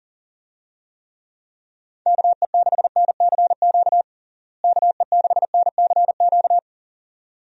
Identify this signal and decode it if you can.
Morse code, heard as a keyed tone, about 40 wpm, 700 Hz. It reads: KE6NCQ KE6NCQ
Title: USA Call Signs - 6 Characters - Repeated 2x - 40wpm